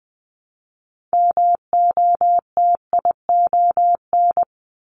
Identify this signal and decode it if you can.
Morse code, heard as a keyed tone, about 20 wpm, 700 Hz. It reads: MOTION
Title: Multi-letter Phonemes Repeated 3x 20wpm